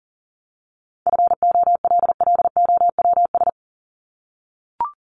Morse code, heard as a keyed tone, about 40 wpm, 700 Hz. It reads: FOLLOWS
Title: ES:S Suffix - Top 100 Words with Base Words 40wpm